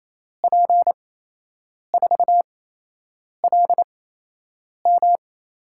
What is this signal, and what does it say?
Morse code, heard as a keyed tone, about 28 wpm, spaced as for 8 wpm, 700 Hz. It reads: P4LM
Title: USA Call Signs - 5 Characters - 8x Character Spacing - 28wpm